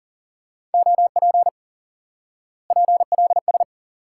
Morse code, heard as a keyed tone, about 40 wpm, 700 Hz. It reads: OP PLS
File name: IQE Course 3 - Lesson 89 - YAGI - Two Elements - 40wpm